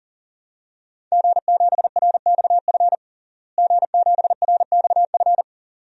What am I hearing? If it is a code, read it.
Morse code, heard as a keyed tone, about 40 wpm, 700 Hz. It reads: G7RXF G7RXF
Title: DXCC-100 Call Signs - 5 Characters - Repeated 2x - 40wpm